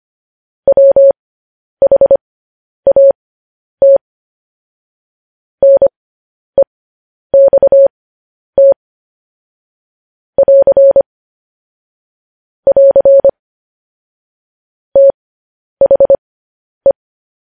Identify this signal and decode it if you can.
Morse code, heard as a keyed tone, about 25 wpm, spaced as for 10 wpm, 550 Hz. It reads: WHAT NEXT <AR> <AR> THE
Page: news headlines in morse code at 10 wpm